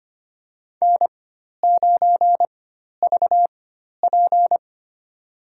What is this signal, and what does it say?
Morse code, heard as a keyed tone, about 25 wpm, spaced as for 12 wpm, 700 Hz. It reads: N9VP
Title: USA Call Signs - 4 Characters - 4x Character Spacing - 25wpm